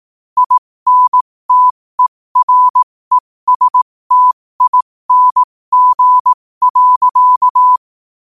Morse code, heard as a keyed tone, about 18 wpm, spaced as for 15 wpm, 1000 Hz. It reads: INTERESTING.
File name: qotdfiles / 15wpm